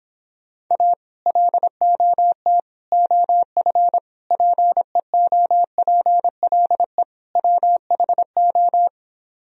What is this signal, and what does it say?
Morse code, heard as a keyed tone, about 26 wpm, 700 Hz. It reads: A LOT OF PEOPLE WHO